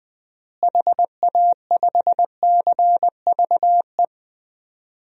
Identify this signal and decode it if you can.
Morse code, heard as a keyed tone, about 20 wpm, 700 Hz. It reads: HA5CVE
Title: DXCC-100 Call Signs - 6 Characters - Repeated 2x - 20wpm